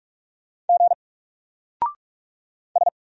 Morse code, heard as a keyed tone, about 44 wpm, 700 Hz. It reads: G S
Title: Single Letter-Number 45wpm